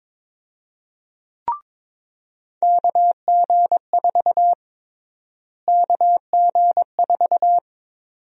Morse code, heard as a keyed tone, about 22 wpm, 700 Hz. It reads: KG4 KG4
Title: USA Call Sign Prefixes - Repeated 2x - 22wpm